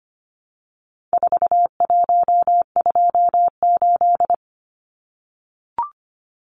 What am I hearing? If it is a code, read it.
Morse code, heard as a keyed tone, about 25 wpm, 700 Hz. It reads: 4128
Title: Numbers - 4-Digit 25wpm